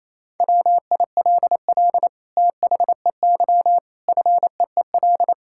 Morse code, heard as a keyed tone, about 28 wpm, 700 Hz. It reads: WILL THEY FEEL
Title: Sentences from Top 1000 Words - Review Part 5 - 28wpm